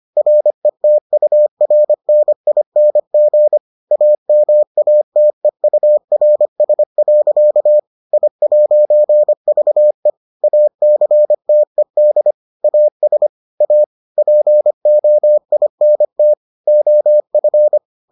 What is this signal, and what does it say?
Morse code, heard as a keyed tone, about 25 wpm, 600 Hz. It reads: RETURNING AMATEURS. I'VE ACTED AS A POINT OF